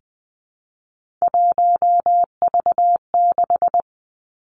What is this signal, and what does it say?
Morse code, heard as a keyed tone, about 20 wpm, 700 Hz. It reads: 1V6
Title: Three Challenging Letter-Number Combinations 20wpm